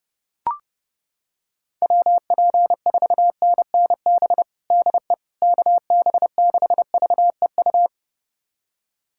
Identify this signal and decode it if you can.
Morse code, heard as a keyed tone, about 29 wpm, 700 Hz. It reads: WP4NNB DE KB6VEU
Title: USA Callsign DE Callsign - 6 Characters - 30wpm